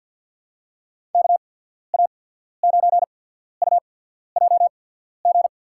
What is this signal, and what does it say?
Morse code, heard as a keyed tone, about 53 wpm, spaced as for 14 wpm, 700 Hz. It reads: KA9UJG
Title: USA Call Signs - 6 Characters - 8x Character Spacing - 50wpm